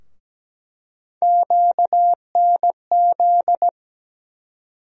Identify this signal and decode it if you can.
Morse code, heard as a keyed tone, about 17 wpm, 700 Hz. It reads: QNZ
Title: USA Call Sign Postfixes - 3 Letters 17wpm